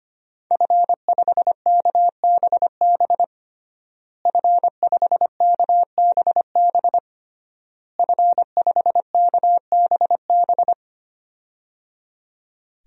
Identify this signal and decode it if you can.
Morse code, heard as a keyed tone, about 25 wpm, 700 Hz. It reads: F5KBB F5KBB F5KBB
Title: DXCC-100 Call Signs - 5 Characters - Repeated 3x - 25wpm